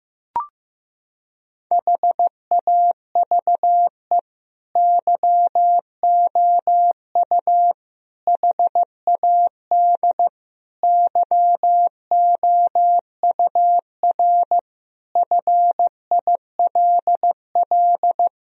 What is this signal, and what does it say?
Morse code, heard as a keyed tone, about 15 wpm, 700 Hz. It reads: HAVE YOU HAD YOUR FILL
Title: Sentences from Top 1000 Words - Review Part 3 - 15wpm